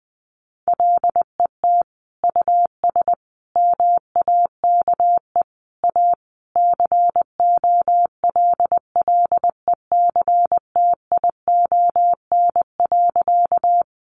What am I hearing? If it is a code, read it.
Morse code, heard as a keyed tone, about 20 wpm, 700 Hz. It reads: LET US MAKE A COLLECTION.
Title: Sentences from Top 1200 Words 20wpm